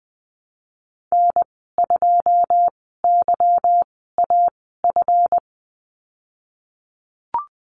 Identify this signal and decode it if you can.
Morse code, heard as a keyed tone, about 20 wpm, spaced as for 14 wpm, 700 Hz. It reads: N2YAF
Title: 161 Lesson 26g - Letter g - Call Signs 20wpm